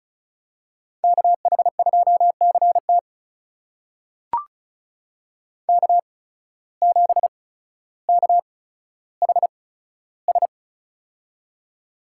Morse code, heard as a keyed tone, about 36 wpm, spaced as for 15 wpm, 700 Hz. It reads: KH2CT K7KHS
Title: USA Call Signs - 5 Characters - 8x Character Spacing - 35wpm